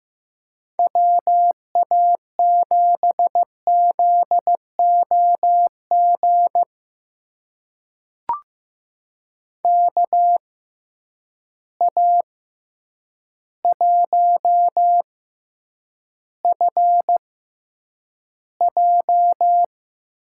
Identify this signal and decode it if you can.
Morse code, heard as a keyed tone, about 15 wpm, spaced as for 8 wpm, 700 Hz. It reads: WA7ZOG KA1FJ
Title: USA Call Signs - 6 Characters - 6x Character Spacing - 15wpm